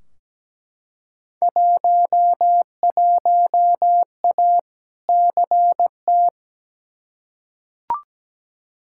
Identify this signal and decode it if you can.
Morse code, heard as a keyed tone, about 17 wpm, 700 Hz. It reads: 11A CT
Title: Field Day Exchanges Encore - 17wpm